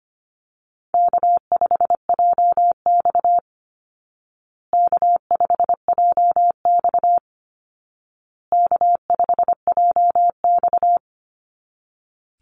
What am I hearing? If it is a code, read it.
Morse code, heard as a keyed tone, about 25 wpm, 700 Hz. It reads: K5JX K5JX K5JX
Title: Call Signs Repeated 2x - 25wpm